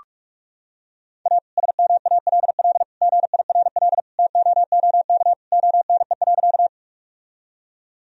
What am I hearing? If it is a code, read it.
Morse code, heard as a keyed tone, about 46 wpm, 700 Hz. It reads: A SMALL GIRL TOOK ONE.
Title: Sentences from Top 1000 Words - Review Part 1 - 45wpm